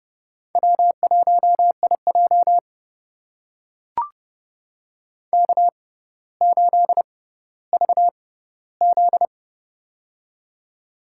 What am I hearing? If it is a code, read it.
Morse code, heard as a keyed tone, about 31 wpm, spaced as for 15 wpm, 700 Hz. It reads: W1IJ K8VZ
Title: USA Call Signs - 4 Characters - 6x Character Spacing - 30wpm